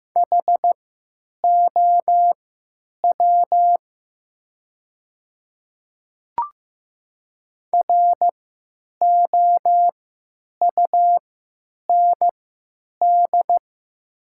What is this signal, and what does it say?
Morse code, heard as a keyed tone, about 15 wpm, spaced as for 9 wpm, 700 Hz. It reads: HOW ROUND
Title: 3x Character Spacing - Top 200 Words - 15wpm